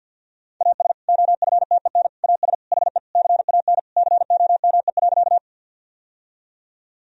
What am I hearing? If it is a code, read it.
Morse code, heard as a keyed tone, about 51 wpm, 700 Hz. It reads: AS OFTEN AS HE CAN COME.
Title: Sentences from Top 1000 Words - Review Part 1 - 50wpm